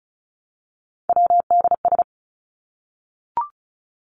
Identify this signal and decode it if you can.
Morse code, heard as a keyed tone, about 34 wpm, 700 Hz. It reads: WDS
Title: Abbreviations 35wpm